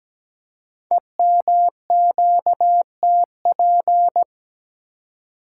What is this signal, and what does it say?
Morse code, heard as a keyed tone, about 17 wpm, 700 Hz. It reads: EMQTP
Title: Code Groups - 5 Letters - 17wpm